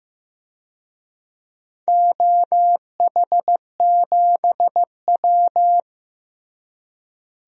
Morse code, heard as a keyed tone, about 15 wpm, 700 Hz. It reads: OH7W
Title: DXCC-100 Call Signs - 4 Characters - Repeated 2x - 15wpm